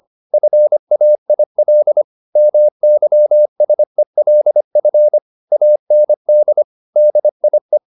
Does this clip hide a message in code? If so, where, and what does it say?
Yes: Morse code from 0.3 to 7.8 s, heard as a keyed tone, about 25 wpm, 600 Hz. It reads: FAIL MYSELF AND DIE